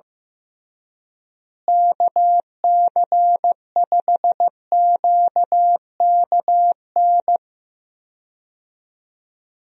Morse code, heard as a keyed tone, about 15 wpm, 700 Hz. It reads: KC5QKN
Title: USA Call Signs - 6 Characters - Repeated 3x - 15wpm